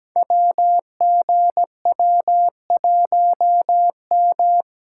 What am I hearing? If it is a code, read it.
Morse code, heard as a keyed tone, about 17 wpm, 700 Hz. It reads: WGW1M
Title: Code Groups - 5 Letter:Numbers - 17wpm